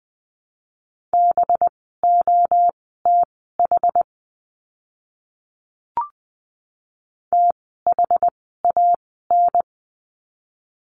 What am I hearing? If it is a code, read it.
Morse code, heard as a keyed tone, about 20 wpm, spaced as for 15 wpm, 700 Hz. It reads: BOTH THAN
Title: 185 Lesson 30f - Letter b - Words 20wpm